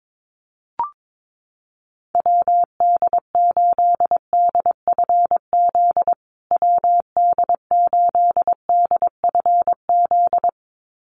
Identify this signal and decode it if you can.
Morse code, heard as a keyed tone, about 22 wpm, 700 Hz. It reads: WD8DFZ WD8DFZ
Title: USA Call Signs Repeated 2x 22wpm